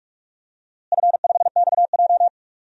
Morse code, heard as a keyed tone, about 46 wpm, 700 Hz. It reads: F5XJ
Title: DXCC-100 Call Signs - 4 Characters - Repeated 2x - 45wpm